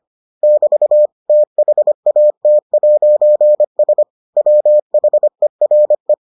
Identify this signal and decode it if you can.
Morse code, heard as a keyed tone, about 25 wpm, 600 Hz. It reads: <BT> THAT'S WHERE